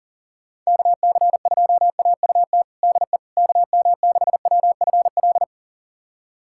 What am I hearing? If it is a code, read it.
Morse code, heard as a keyed tone, about 40 wpm, 700 Hz. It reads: KC2AUT DE KM6WFL
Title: USA Callsign DE Callsign - 6 Characters - 40wpm